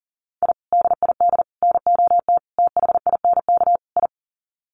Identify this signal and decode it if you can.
Morse code, heard as a keyed tone, about 40 wpm, 700 Hz. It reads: I DID NOT THINK I
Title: Sets of 5 Words Encore 40wpm